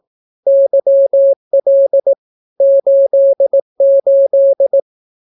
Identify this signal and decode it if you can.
Morse code, heard as a keyed tone, about 18 wpm, 550 Hz. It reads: YL 88
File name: naukapodstaw / skroty / skroty